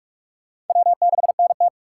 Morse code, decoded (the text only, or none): W6NT